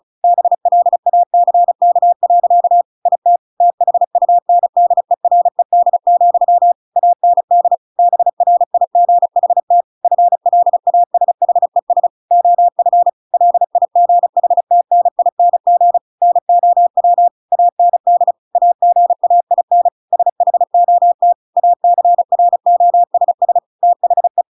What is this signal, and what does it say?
Morse code, heard as a keyed tone, about 34 wpm, 700 Hz. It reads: BLACK. IT THUNDERED, AND BRIGHT FLASHES OF LIGHTNING NOW AND AGAIN SHOT ACROSS THE
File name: Pinocchio / Pinocchio ch6 35 35 ik1hge